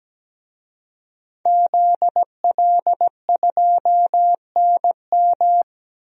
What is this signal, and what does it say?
Morse code, heard as a keyed tone, about 17 wpm, 700 Hz. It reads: ZL2NM
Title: DXCC-100 Call Signs - 5 Characters - 17wpm